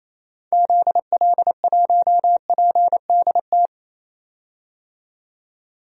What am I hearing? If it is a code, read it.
Morse code, heard as a keyed tone, about 28 wpm, 700 Hz. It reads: ZL1PDT
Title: DXCC-100 Call Signs - 6 Characters - 28wpm